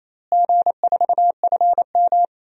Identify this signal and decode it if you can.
Morse code, heard as a keyed tone, about 28 wpm, 700 Hz. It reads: G4FM
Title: DXCC-100 Call Signs - 4 Characters - Repeated 2x - 28wpm